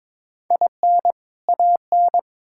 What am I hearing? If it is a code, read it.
Morse code, heard as a keyed tone, about 21 wpm, 700 Hz. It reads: IN AN
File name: Build A Phrase - 2-Word Phrases - 22wpm